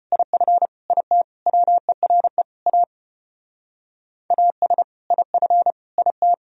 Morse code, heard as a keyed tone, about 33 wpm, 700 Hz. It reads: IF IT WERE A AS IF IT